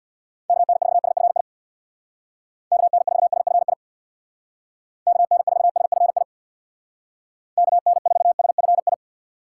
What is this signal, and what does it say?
Morse code, heard as a keyed tone, about 65 wpm, 700 Hz. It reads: KN4SFI KN4SFI KN4SFI KN4SFI